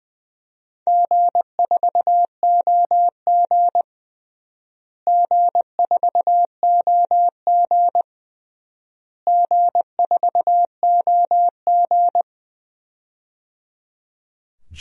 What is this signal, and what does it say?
Morse code, heard as a keyed tone, about 20 wpm, 700 Hz. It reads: G4OG G4OG G4OG
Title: DXCC-100 Call Signs - 4 Characters - Repeated 3x - 20wpm